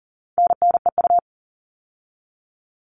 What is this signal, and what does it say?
Morse code, heard as a keyed tone, about 40 wpm, 700 Hz. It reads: NNEU
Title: All Cut Numbers - 4-Digit 40wpm